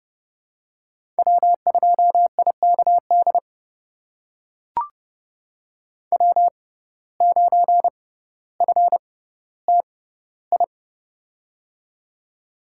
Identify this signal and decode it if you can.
Morse code, heard as a keyed tone, about 30 wpm, spaced as for 15 wpm, 700 Hz. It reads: W2IKD W9FTI
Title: USA Call Signs - 5 Characters - 6x Character Spacing - 30wpm